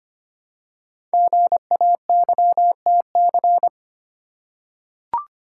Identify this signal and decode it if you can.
Morse code, heard as a keyed tone, about 25 wpm, 700 Hz. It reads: GAYTC
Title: Code Groups - 5 Letters - 25wpm